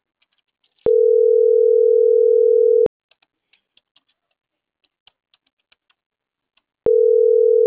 Ringback tone (North America), 440 + 480 Hz.